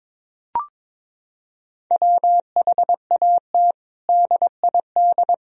Morse code, heard as a keyed tone, about 22 wpm, 700 Hz. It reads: WHAT DID
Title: Sentences from Top 1000 Words - Review Part 3 - 22wpm